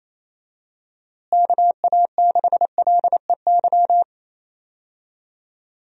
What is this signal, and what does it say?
Morse code, heard as a keyed tone, about 27 wpm, 700 Hz. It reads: KA6LEY
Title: USA Call Signs - 6 Characters - Repeated 2x - 28wpm